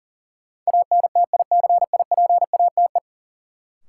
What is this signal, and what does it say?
Morse code, heard as a keyed tone, about 37 wpm, 700 Hz. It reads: ANTICIPATE
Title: Sentences from Top 1500 Words - New Word Drill - 40wpm